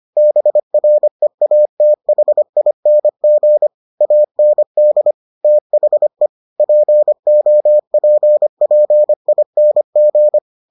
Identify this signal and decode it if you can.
Morse code, heard as a keyed tone, about 25 wpm, 600 Hz. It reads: BREATHING AND THE POPPING